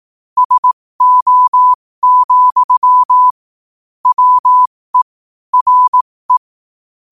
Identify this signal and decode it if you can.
Morse code, heard as a keyed tone, about 18 wpm, spaced as for 15 wpm, 1000 Hz. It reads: SO, WE RE
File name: qotdfiles / 15wpm